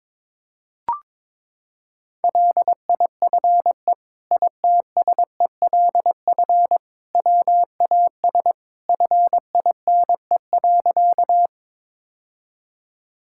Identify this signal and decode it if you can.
Morse code, heard as a keyed tone, about 22 wpm, 700 Hz. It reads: LIFE ITSELF WAS FINE.
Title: Sentences from Top 1000 Words - Review Part 5 - 22wpm